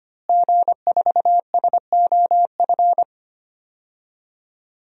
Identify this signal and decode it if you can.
Morse code, heard as a keyed tone, about 25 wpm, 700 Hz. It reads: G4SOF
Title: DXCC-100 Call Signs - 5 Characters - Repeated 2x - 25wpm